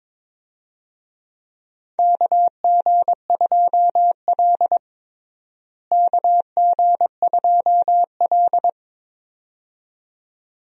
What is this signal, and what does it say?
Morse code, heard as a keyed tone, about 22 wpm, 700 Hz. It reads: KG2L KG2L
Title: USA Call Signs - 4 Characters - Repeated 2x - 22wpm